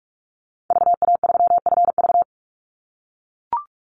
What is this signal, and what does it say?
Morse code, heard as a keyed tone, about 44 wpm, 700 Hz. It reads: VA3FV
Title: Call Signs - 45wpm